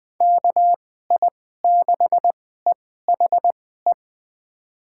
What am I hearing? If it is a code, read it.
Morse code, heard as a keyed tone, about 20 wpm, spaced as for 14 wpm, 700 Hz. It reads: KI6EHE
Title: USA Call Signs - 6 Characters - 2x Character Spacing - 20wpm